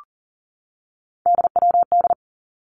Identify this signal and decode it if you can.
Morse code, heard as a keyed tone, about 43 wpm, 700 Hz. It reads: DWD
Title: Three Challenging Letter-Number Combinations 40wpm